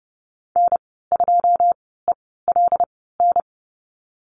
Morse code, heard as a keyed tone, about 30 wpm, spaced as for 17 wpm, 700 Hz. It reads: N2ELN
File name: Lesson 15g - Number 5 - Call Signs 30wpm